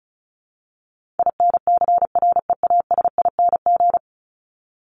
Morse code, heard as a keyed tone, about 34 wpm, 700 Hz. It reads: INCREASING
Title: ING Suffix - Top 100 Words with Base Words 35wpm